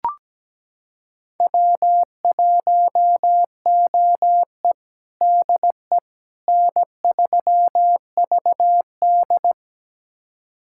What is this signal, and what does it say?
Morse code, heard as a keyed tone, about 17 wpm, 700 Hz. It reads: W1OE DE N3VD
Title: USA Callsign DE Callsign - 4 Characters - 17wpm